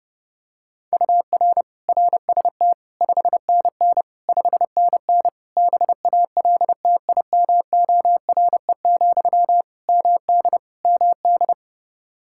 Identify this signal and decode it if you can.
Morse code, heard as a keyed tone, about 30 wpm, 700 Hz. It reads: UR RST 5NN 5NN BALTIMORE, MD MD
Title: Practice QSO - Exchange 1 - Beginning - 30wpm